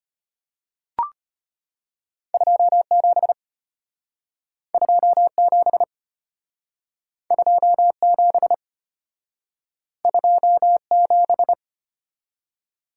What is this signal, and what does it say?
Morse code, heard as a keyed tone, about 33 wpm, 700 Hz. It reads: 27 27 27 27